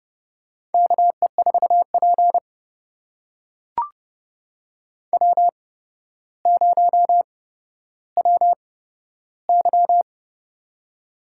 Morse code, heard as a keyed tone, about 30 wpm, spaced as for 13 wpm, 700 Hz. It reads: KE4P W0WY
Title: USA Call Signs - 4 Characters - 8x Character Spacing - 30wpm